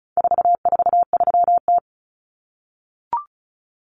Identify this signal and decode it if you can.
Morse code, heard as a keyed tone, about 34 wpm, 700 Hz. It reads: 443T
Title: Common Cut Numbers - 4-Digit 35wpm